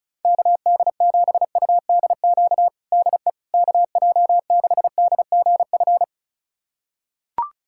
Morse code, heard as a keyed tone, about 34 wpm, 700 Hz. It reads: KD7UDQ DE KJ6DGF
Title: USA Callsign DE Callsign - 6 Characters - 35wpm